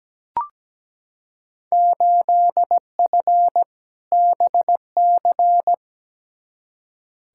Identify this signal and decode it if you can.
Morse code, heard as a keyed tone, about 17 wpm, 700 Hz. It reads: 8F BC
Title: Field Day Exchanges Encore - 17wpm